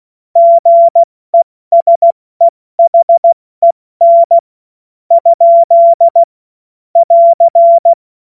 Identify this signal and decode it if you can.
Morse code, heard as a keyed tone, about 16 wpm, spaced as for 13 wpm, 680 Hz. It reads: GESEHEN? <AR>
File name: Morsekurs / Witze / Witz 13 Februar 2024 70 F80